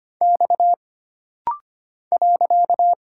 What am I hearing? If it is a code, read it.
Morse code, heard as a keyed tone, about 25 wpm, 700 Hz. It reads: X .
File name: Single Character (Letter:Number:Punctuation:Prosign) - Rapid-Fire - 25wpm